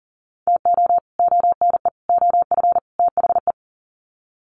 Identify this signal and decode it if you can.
Morse code, heard as a keyed tone, about 40 wpm, 700 Hz. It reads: TO ONE OF THE
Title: Sets of 4 Words 40wpm